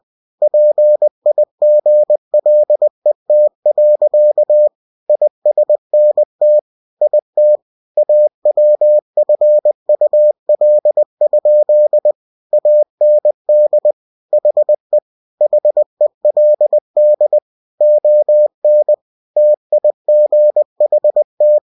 Morse code, heard as a keyed tone, about 20 wpm, 600 Hz. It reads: PIGLET. ISNT IT AWFUL? AND HE HELD ON TIGHT
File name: cw-books / winnie-the-pooh / winnie-20wpm-page-0043